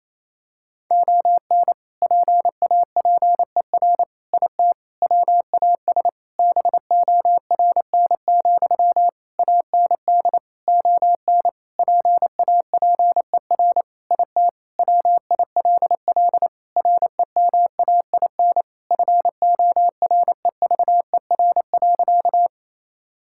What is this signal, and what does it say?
Morse code, heard as a keyed tone, about 28 wpm, 700 Hz. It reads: ON PAPER IT WAS BORN, AND ON PAPER IT WILL REMAIN FOREVER.
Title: Sentences from Top 1600 Words 28wpm